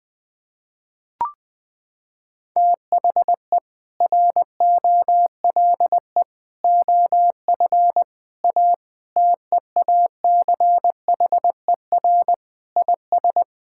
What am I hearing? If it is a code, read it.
Morse code, heard as a keyed tone, about 20 wpm, 700 Hz. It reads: THE ROLE OF A TEACHER IS
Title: Sentences with 3-Word Phrases - 20wpm